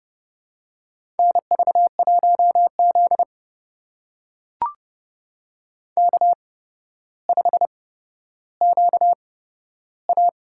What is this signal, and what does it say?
Morse code, heard as a keyed tone, about 30 wpm, spaced as for 13 wpm, 700 Hz. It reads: NV1Z K5QA
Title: USA Call Signs - 8x Character Spacing - 30wpm